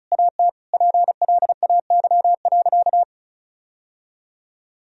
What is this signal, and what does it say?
Morse code, heard as a keyed tone, about 36 wpm, 700 Hz. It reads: AT PLAY.
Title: Sentences from Top 1000 Words - Review Part 4 - 35wpm